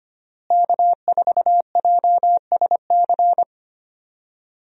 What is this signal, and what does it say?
Morse code, heard as a keyed tone, about 25 wpm, 700 Hz. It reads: K4JSC
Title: USA Call Signs - 5 Characters - Repeated 3x - 25wpm